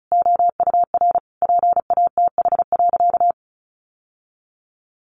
Morse code, heard as a keyed tone, about 34 wpm, 700 Hz. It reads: OUR PATH.
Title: Sentences from Top 1400 Words 35wpm